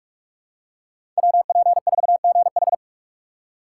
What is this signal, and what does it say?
Morse code, heard as a keyed tone, about 46 wpm, 700 Hz. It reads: WP4GH
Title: USA Call Signs - 5 Characters - 4x Character Spacing - 45wpm